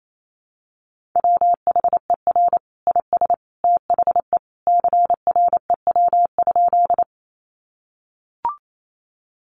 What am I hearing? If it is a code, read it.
Morse code, heard as a keyed tone, about 28 wpm, 700 Hz. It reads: WHER IS THE CREW?
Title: Sentences from Top 1500 Words 28wpm